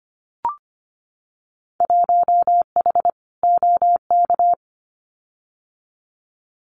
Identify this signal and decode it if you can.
Morse code, heard as a keyed tone, about 24 wpm, 700 Hz. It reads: 1H OK
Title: Winter Field Day Exchanges 25wpm